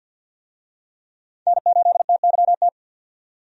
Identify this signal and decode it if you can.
Morse code, heard as a keyed tone, about 51 wpm, 700 Hz. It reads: N8TYT
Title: USA Call Signs - 5 Characters - 50wpm